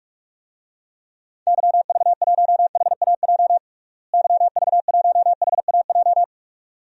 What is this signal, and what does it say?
Morse code, heard as a keyed tone, about 42 wpm, 700 Hz. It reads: YV1HAJ YV1HAJ
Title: DXCC-100 Call Signs - 6 Characters - Repeated 2x - 45wpm